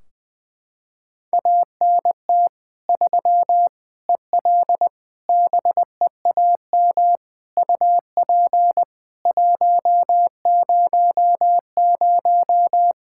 Morse code, heard as a keyed tone, about 20 wpm, 700 Hz. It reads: ANT 3 EL BEAM UP 100
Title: Practice QSO - Exchange 2 - Info - 20wpm